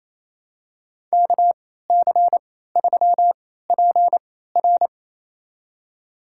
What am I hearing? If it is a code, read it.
Morse code, heard as a keyed tone, about 28 wpm, spaced as for 16 wpm, 700 Hz. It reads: KC3PR
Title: USA Call Signs - 3x Character Spacing - 28wpm